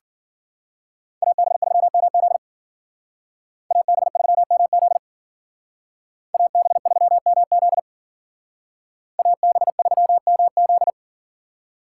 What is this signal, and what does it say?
Morse code, heard as a keyed tone, about 48 wpm, 700 Hz. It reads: AB3MZ AB3MZ AB3MZ AB3MZ
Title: Speed Racing - USA Call Signs 40wpm